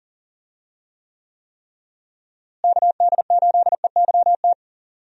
Morse code, heard as a keyed tone, about 40 wpm, 700 Hz. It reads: KD8EYT
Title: USA Call Signs - 6 Characters - Repeated 3x - 40wpm